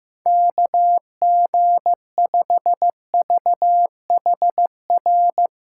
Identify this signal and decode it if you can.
Morse code, heard as a keyed tone, about 16 wpm, 700 Hz. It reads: KG5VHR
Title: USA Call Signs - 6 Characters - 4x Character Spacing - 15wpm